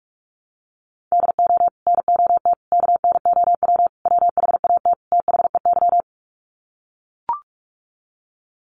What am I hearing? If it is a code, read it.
Morse code, heard as a keyed tone, about 44 wpm, 700 Hz. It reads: DO NOT KNOW WHAT THEY
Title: Sets of 5 Words 45wpm